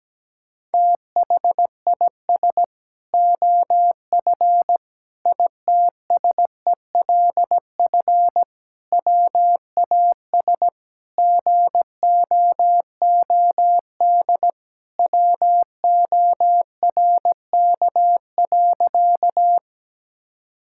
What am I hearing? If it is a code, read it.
Morse code, heard as a keyed tone, about 17 wpm, 700 Hz. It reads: THIS OF ITSELF WAS GOOD WORK.
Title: Sentences from Top 1000 Words - Review Part 5 - 17wpm